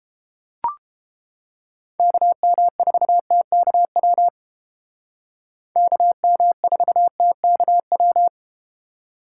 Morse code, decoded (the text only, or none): KM4TKW KM4TKW